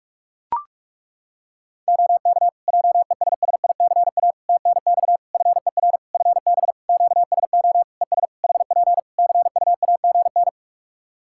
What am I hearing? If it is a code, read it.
Morse code, heard as a keyed tone, about 42 wpm, 700 Hz. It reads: OK JESSICA TNX FER FB QSO ES HP CUAGN
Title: Practice QSO - Exchange 4 - Ending - 45wpm